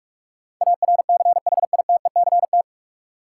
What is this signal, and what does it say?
Morse code, heard as a keyed tone, about 46 wpm, 700 Hz. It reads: ARCHITECT
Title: K1USN SST QSO Practice 45wpm